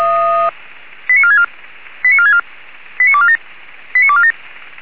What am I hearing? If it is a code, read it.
Five-tone selective call (ZVEI), heard as quick runs of five tones, tones of about 70 ms. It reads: 89464 89464 89247 89247